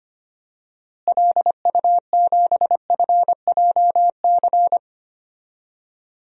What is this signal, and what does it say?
Morse code, heard as a keyed tone, about 25 wpm, 700 Hz. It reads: LU7FJC
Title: DXCC-100 Call Signs - 6 Characters - Repeated 3x - 25wpm